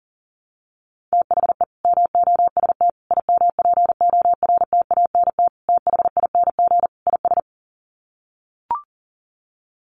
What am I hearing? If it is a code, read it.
Morse code, heard as a keyed tone, about 40 wpm, 700 Hz. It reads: THE MOST IMPORTANT THING IS
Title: Sets of 5 Words Encore 40wpm